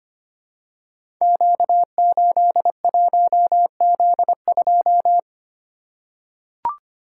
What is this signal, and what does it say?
Morse code, heard as a keyed tone, about 26 wpm, 700 Hz. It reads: Q81Z2
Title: Code Groups - 5 Letter:Numbers - 25wpm